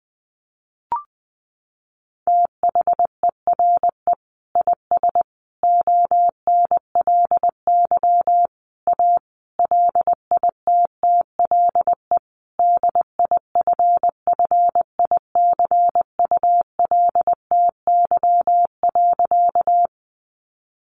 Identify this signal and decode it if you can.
Morse code, heard as a keyed tone, about 20 wpm, 700 Hz. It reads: THERE IS ONLY A LITTLE DIFFICULTY.
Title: Sentences from Top 1500 Words 20wpm